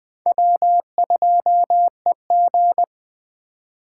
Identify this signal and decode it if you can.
Morse code, heard as a keyed tone, about 20 wpm, 700 Hz. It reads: W2EG
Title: Speed Racing - USA Call Signs 17wpm